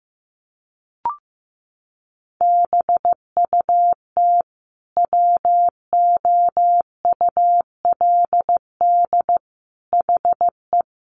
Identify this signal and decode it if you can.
Morse code, heard as a keyed tone, about 15 wpm, 700 Hz. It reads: BUT WOULD HE
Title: Sentences from Top 1400 Words 15wpm